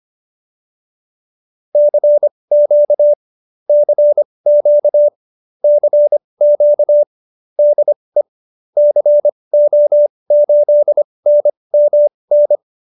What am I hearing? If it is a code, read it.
Morse code, heard as a keyed tone, about 25 wpm, spaced as for 20 wpm, 590 Hz. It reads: CQ CQ CQ DE CO8NMN